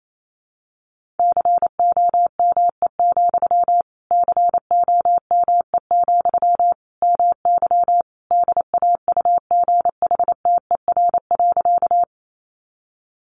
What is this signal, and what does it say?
Morse code, heard as a keyed tone, about 28 wpm, 700 Hz. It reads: COME, COME, MY DAUGHTER.
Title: Sentences from Top 1500 Words 28wpm